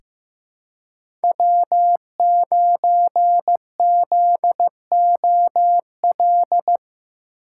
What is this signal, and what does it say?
Morse code, heard as a keyed tone, about 15 wpm, 700 Hz. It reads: W9ZOL
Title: USA Call Signs - 5 Characters - 3x Character Spacing - 15wpm